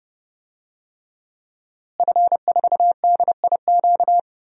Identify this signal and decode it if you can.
Morse code, heard as a keyed tone, about 32 wpm, 700 Hz. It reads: F4DIQ
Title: DXCC-100 Call Signs - 5 Characters - Repeated 3x - 30wpm